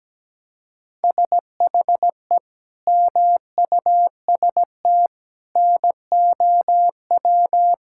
Morse code, heard as a keyed tone, about 17 wpm, 700 Hz. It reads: SHE MUST NOW
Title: Sentences from Top 1600 Words 17wpm